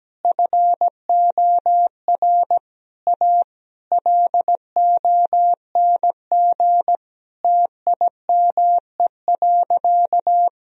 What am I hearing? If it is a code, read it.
Morse code, heard as a keyed tone, about 17 wpm, 700 Hz. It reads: FOR A LONG TIME.
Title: Sentences from Top 1000 Words - Review Part 2 - 17wpm